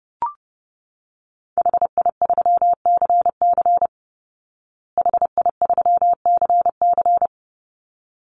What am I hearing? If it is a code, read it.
Morse code, heard as a keyed tone, about 30 wpm, 700 Hz. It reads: HI3CC HI3CC
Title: Call Signs Repeated 3x - 30wpm